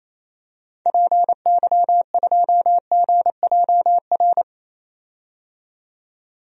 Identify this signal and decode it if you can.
Morse code, heard as a keyed tone, about 28 wpm, 700 Hz. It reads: PY2GJR
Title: DXCC-100 Call Signs - 6 Characters - 28wpm